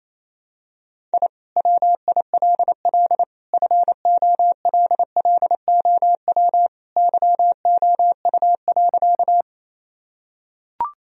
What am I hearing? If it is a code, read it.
Morse code, heard as a keyed tone, about 28 wpm, 700 Hz. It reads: I WILL FOLLOW YOU.
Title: Sentences from Top 1000 Words - Review Part 1 - 28wpm